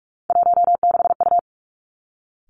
Morse code, heard as a keyed tone, about 46 wpm, 700 Hz. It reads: 16U
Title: Three Challenging Letter-Number Combinations 45wpm